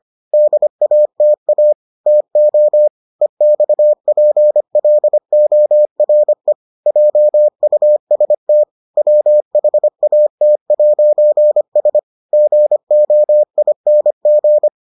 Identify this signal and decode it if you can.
Morse code, heard as a keyed tone, about 25 wpm, 600 Hz. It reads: DATA TO EXPLORE JUST WHAT'S GOING